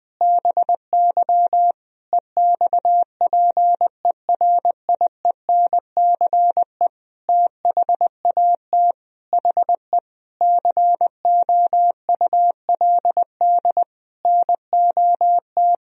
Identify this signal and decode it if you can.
Morse code, heard as a keyed tone, about 21 wpm, 700 Hz. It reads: BY EXPERIENCE THAT HE COULD NOT
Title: Sentences from Top 1000 Words - Review Part 3 - 20wpm